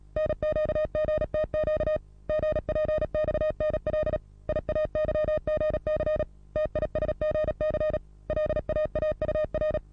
Morse code, heard as a keyed tone, about 37 wpm, 620 Hz. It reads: NQGTQ GPXNL IAYGC TISGC LAAUR
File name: repos / Morsekurs / DJ4UF Morsen27